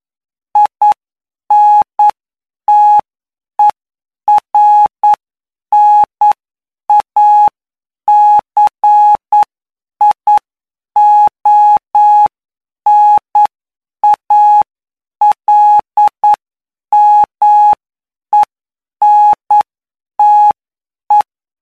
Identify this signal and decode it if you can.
Morse code, heard as a keyed tone, about 9 wpm, spaced as for 7 wpm, 840 Hz. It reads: INTERNACIONALMENTE
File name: cw / py2lcd / 20-PY2LCD-Li��o 29